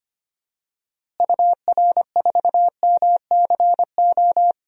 Morse code, heard as a keyed tone, about 25 wpm, 700 Hz. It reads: UR4MCO
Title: DXCC-100 Call Signs - 6 Characters - Repeated 2x - 25wpm